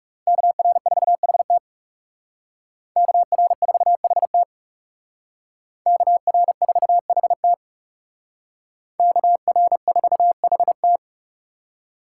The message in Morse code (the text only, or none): KR4HT KR4HT KR4HT KR4HT